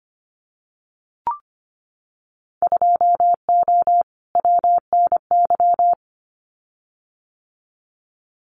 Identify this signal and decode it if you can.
Morse code, heard as a keyed tone, about 25 wpm, 700 Hz. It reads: 2O WNY
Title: Winter Field Day Exchanges Encore 25wpm